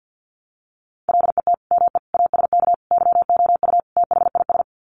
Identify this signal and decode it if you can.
Morse code, heard as a keyed tone, about 50 wpm, 700 Hz. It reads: LET ME ASK YOU THIS
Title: Sets of 5 Words 50wpm